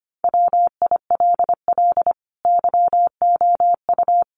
Morse code, heard as a keyed tone, about 25 wpm, 700 Hz. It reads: WILL YOU
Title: Sentences from Top 1200 Words 25wpm